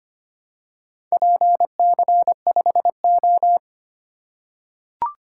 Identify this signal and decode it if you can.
Morse code, heard as a keyed tone, about 25 wpm, 700 Hz. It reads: PC5O
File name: practice / 25wpm / Four Letter-Number Combinations - 25wpm